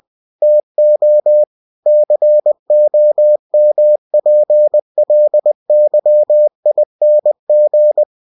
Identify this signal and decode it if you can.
Morse code, heard as a keyed tone, about 20 wpm, 600 Hz. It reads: TO COMPLYING
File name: cw-books / winnie-the-pooh / winnie-20wpm-page-0121